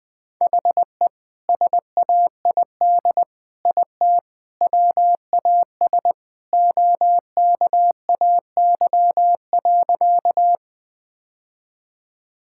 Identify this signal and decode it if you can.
Morse code, heard as a keyed tone, about 20 wpm, 700 Hz. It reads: HE SAID IT WAS OKAY.
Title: Sentences with 2-Word Phrases - 20wpm